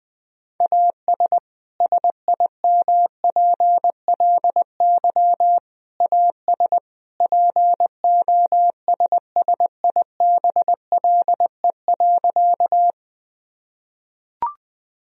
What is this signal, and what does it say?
Morse code, heard as a keyed tone, about 21 wpm, 700 Hz. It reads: AS SIMPLY AS POSSIBLE.
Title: Sentences from Top 1000 Words - Review Part 4 - 20wpm